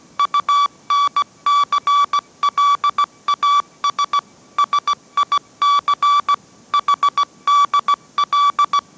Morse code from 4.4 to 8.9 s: SIC HDL